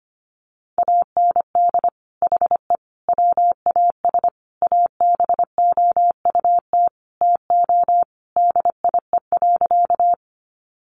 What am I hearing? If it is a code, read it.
Morse code, heard as a keyed tone, about 25 wpm, 700 Hz. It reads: AND HE WAS ABOUT TO DIE.
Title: Sentences from Top 1300 Words 25wpm